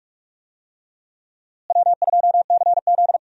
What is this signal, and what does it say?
Morse code, heard as a keyed tone, about 43 wpm, 700 Hz. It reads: W2CZ
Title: USA Call Signs - 4 Characters - Repeated 2x - 45wpm